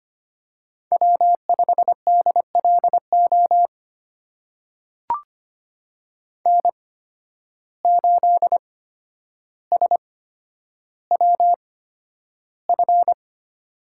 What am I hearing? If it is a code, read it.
Morse code, heard as a keyed tone, about 25 wpm, spaced as for 11 wpm, 700 Hz. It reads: W5DLO N8SWF
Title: USA Call Signs - 5 Characters - 8x Character Spacing - 25wpm